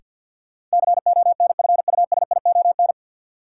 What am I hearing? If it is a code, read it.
Morse code, heard as a keyed tone, about 52 wpm, 700 Hz. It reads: CONFUSION
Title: Sentences from Top 1600 Words - New Word Drill - 50wpm